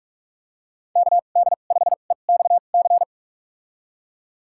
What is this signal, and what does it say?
Morse code, heard as a keyed tone, about 47 wpm, spaced as for 33 wpm, 700 Hz. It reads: KD5EXC
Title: USA Call Signs - 2x Character Spacing - 45wpm